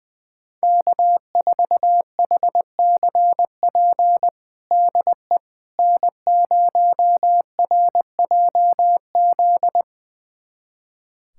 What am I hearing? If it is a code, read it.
Morse code, heard as a keyed tone, about 20 wpm, 700 Hz. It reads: K4HCP DE N0RJZ
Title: USA Callsign DE Callsign - 5 Characters - 20wpm